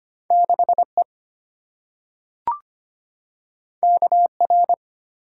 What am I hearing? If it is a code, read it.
Morse code, heard as a keyed tone, about 25 wpm, 700 Hz. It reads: 6E KR
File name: Two Letter-Number Combinations - 25wpm